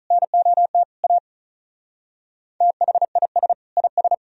Morse code, heard as a keyed tone, about 44 wpm, 700 Hz. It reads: NOT A THIS IS